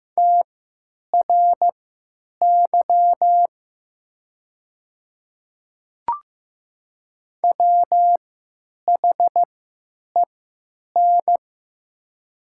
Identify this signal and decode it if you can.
Morse code, heard as a keyed tone, about 15 wpm, spaced as for 9 wpm, 700 Hz. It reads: TRY WHEN
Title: 3x Character Spacing - Top 200 Words - 15wpm